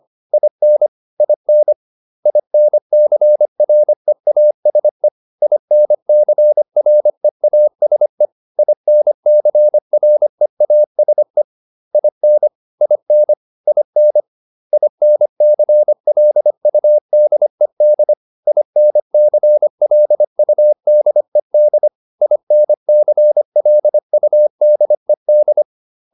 Morse code, heard as a keyed tone, about 25 wpm, 600 Hz. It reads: IN IN INCREASE INCREASE INCREASE IN IN IN INCLUDED INCLUDED INCLUDED